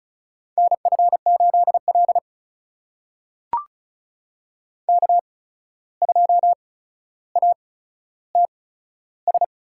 Morse code, heard as a keyed tone, about 34 wpm, spaced as for 14 wpm, 700 Hz. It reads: NF8L K2ATS
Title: USA Call Signs - 8x Character Spacing - 35wpm